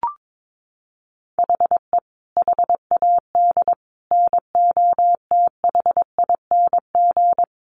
Morse code, heard as a keyed tone, about 22 wpm, 700 Hz. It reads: HE HAD NOTHING
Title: Sentences from Top 1100 Words 22wpm